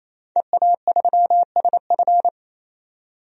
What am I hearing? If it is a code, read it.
Morse code, heard as a keyed tone, about 27 wpm, 700 Hz. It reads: EA3SF
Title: DXCC-100 Call Signs - 5 Characters - 28wpm